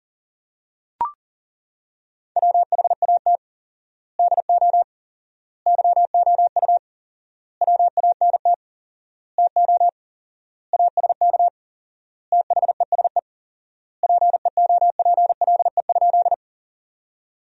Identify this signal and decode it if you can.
Morse code, heard as a keyed tone, about 40 wpm, 700 Hz. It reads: WHAT DO YOU WANT TO ASK THESE PEOPLE?
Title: 4x Word-Spacing - Sentences from Top 200 Words - 40wpm